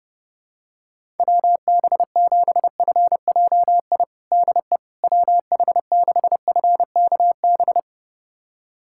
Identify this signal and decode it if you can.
Morse code, heard as a keyed tone, about 30 wpm, 700 Hz. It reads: WB7FJI DE WH6FKB
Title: USA Callsign DE Callsign - 6 Characters - 30wpm